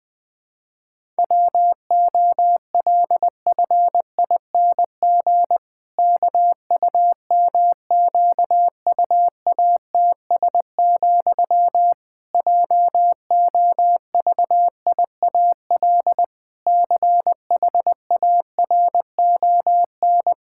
Morse code, heard as a keyed tone, about 20 wpm, 700 Hz. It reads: WOLFING KUMQUATS, JOVIAL CHARON
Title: Pangram Challenge v2 - 20wpm